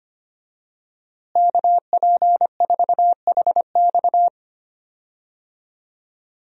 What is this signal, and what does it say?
Morse code, heard as a keyed tone, about 25 wpm, 700 Hz. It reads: KP4HX
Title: USA Call Signs - 5 Characters - Repeated 3x - 25wpm